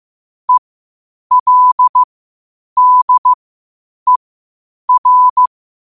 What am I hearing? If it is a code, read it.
Morse code, heard as a keyed tone, about 15 wpm, spaced as for 8 wpm, 1000 Hz. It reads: ELDER